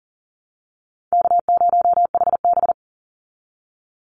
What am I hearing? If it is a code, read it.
Morse code, heard as a keyed tone, about 40 wpm, 700 Hz. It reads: K0HB
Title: Call Signs Repeated 2x - 40wpm